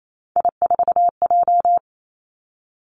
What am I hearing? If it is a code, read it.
Morse code, heard as a keyed tone, about 28 wpm, 700 Hz. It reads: I4J
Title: Three Challenging Letter-Number Combinations 28wpm